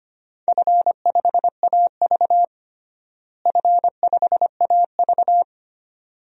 Morse code, heard as a keyed tone, about 25 wpm, 700 Hz. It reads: F5AV F5AV
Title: DXCC-100 Call Signs - 4 Characters - Repeated 2x - 25wpm